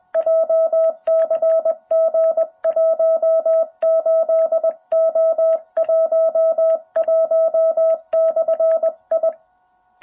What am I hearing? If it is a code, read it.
Morse code, heard as a keyed tone, about 21 wpm, 660 Hz. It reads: JCG 18O11/I
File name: jh8cbh / 2021cw